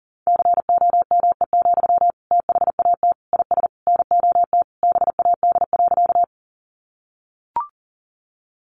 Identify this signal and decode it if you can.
Morse code, heard as a keyed tone, about 40 wpm, 700 Hz. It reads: COME, THAT IS NOT BAD.
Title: Sentences from Top 700 Words 40wpm